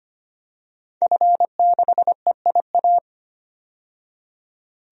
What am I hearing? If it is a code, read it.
Morse code, heard as a keyed tone, about 25 wpm, 700 Hz. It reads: F6EIA
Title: DXCC-100 Call Signs - 5 Characters - 25wpm